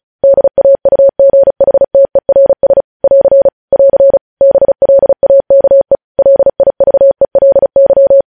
Morse code, heard as a keyed tone, about 36 wpm, 550 Hz. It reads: DAUGHTERS <AR> <AR> BLAKE LIVELY
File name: podcast / News headlines-1x-35x35